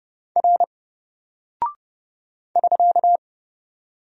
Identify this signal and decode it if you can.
Morse code, heard as a keyed tone, about 30 wpm, 700 Hz. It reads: R <SK>
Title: Single Character (Letter:Number:Punctuation:Prosign) 30wpm